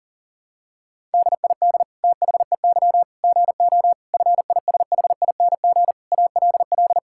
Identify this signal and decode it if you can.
Morse code, heard as a keyed tone, about 40 wpm, 700 Hz. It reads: DID THEY GO FISHING ALL
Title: Sentences from Top 1000 Words - Review Part 5 - 40wpm